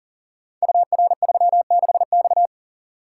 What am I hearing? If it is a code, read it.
Morse code, heard as a keyed tone, about 38 wpm, 700 Hz. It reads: UR36X
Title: Code Groups - 5 Letter:Numbers - 40wpm